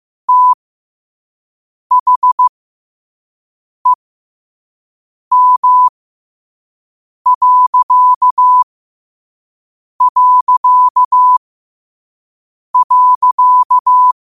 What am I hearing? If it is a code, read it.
Morse code, heard as a keyed tone, about 15 wpm, spaced as for 5 wpm, 1000 Hz. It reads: THEM...
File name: qotdfiles / 5wpm